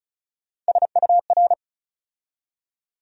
Morse code, heard as a keyed tone, about 36 wpm, 700 Hz. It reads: SUR
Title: USA Call Sign Postfixes - 3 Letters 35wpm